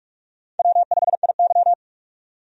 Morse code, heard as a keyed tone, about 45 wpm, 700 Hz. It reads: W5IY
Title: USA Call Signs - 4 Characters - 45wpm